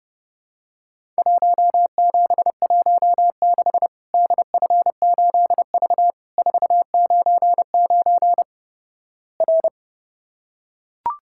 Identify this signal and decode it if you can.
Morse code, heard as a keyed tone, about 30 wpm, 700 Hz. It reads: 1716 DF8V 499 R
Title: Worked All Europe DX Contest - QTC Exchange - 30wpm